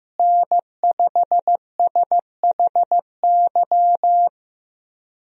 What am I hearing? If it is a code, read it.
Morse code, heard as a keyed tone, about 15 wpm, 700 Hz. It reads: N5SHY
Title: USA Call Signs - 5 Characters - Repeated 2x - 15wpm